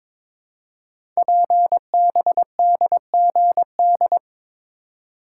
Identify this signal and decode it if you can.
Morse code, heard as a keyed tone, about 23 wpm, 700 Hz. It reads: PBDGD
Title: Code Groups - 5 Letters - 22wpm